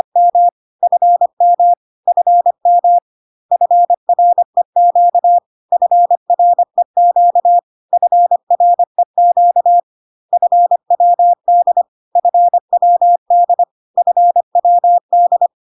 Morse code, heard as a keyed tone, about 25 wpm, 700 Hz. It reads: M FM FM FREQ FREQ FREQ FWD FWD FWD
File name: cw / CW common abbreviations alphabetical repeated 3 times 25 25wpm ik1hge